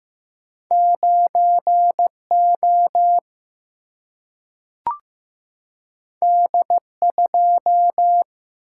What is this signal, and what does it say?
Morse code, heard as a keyed tone, about 15 wpm, 700 Hz. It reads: 9O D2
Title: Single-Letter Single-Number 15wpm